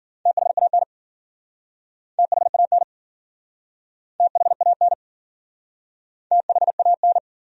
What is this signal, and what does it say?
Morse code, heard as a keyed tone, about 53 wpm, 700 Hz. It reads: THAN THAN THAN THAN